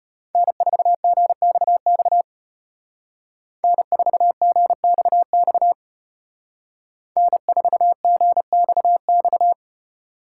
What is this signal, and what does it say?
Morse code, heard as a keyed tone, about 30 wpm, 700 Hz. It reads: N4GXX N4GXX N4GXX
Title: Speed Racing - USA 1x3 Call Signs 25wpm